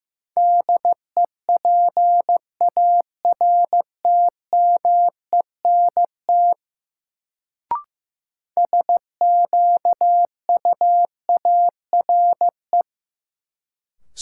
Morse code, heard as a keyed tone, about 15 wpm, 700 Hz. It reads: DEPARTMENT SQUARE
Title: Sentences from Top 1200 Words - New Word Drill Alt - 15wpm